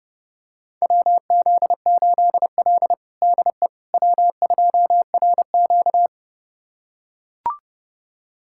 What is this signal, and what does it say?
Morse code, heard as a keyed tone, about 32 wpm, 700 Hz. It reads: WZ8L DE W2RQ
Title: USA Callsign DE Callsign - 4 Characters - 30wpm